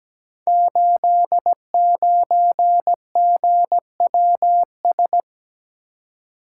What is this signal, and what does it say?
Morse code, heard as a keyed tone, about 17 wpm, 700 Hz. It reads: 89GWS
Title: Code Groups - 5 Letter:Numbers - 17wpm